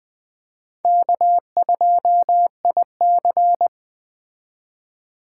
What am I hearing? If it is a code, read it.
Morse code, heard as a keyed tone, about 20 wpm, 700 Hz. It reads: K2IC
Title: Speed Racing - USA Call Signs 17wpm